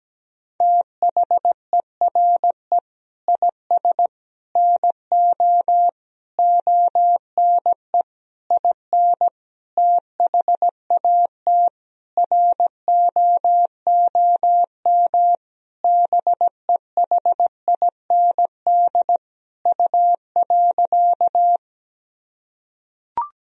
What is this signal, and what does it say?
Morse code, heard as a keyed tone, about 17 wpm, 700 Hz. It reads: THERE IS NO ONE IN THAT ROOM BEHIND U.
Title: Sentences from Top 500 Words and Abbreviations 17wpm